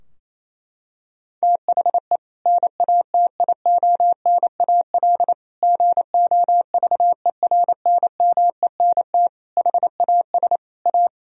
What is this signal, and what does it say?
Morse code, heard as a keyed tone, about 28 wpm, 700 Hz. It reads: THE NATIONAL GOVERNMENT HAS A
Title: Sentences from Top 1000 Words - Review Part 4 - 28wpm